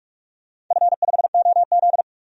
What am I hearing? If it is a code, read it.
Morse code, heard as a keyed tone, about 42 wpm, 700 Hz. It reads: F5OZ
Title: DXCC-100 Call Signs - 4 Characters - 45wpm